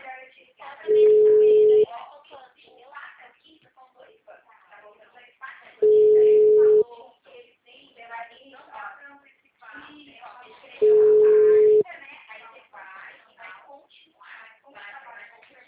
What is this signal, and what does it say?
Ringback tone (Europe, 425 Hz), with speech.